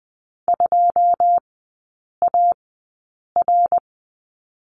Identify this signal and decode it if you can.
Morse code, heard as a keyed tone, about 20 wpm, spaced as for 8 wpm, 700 Hz. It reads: 2AR
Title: 114 Lesson 19e - Letter w - Three Characters 20wpm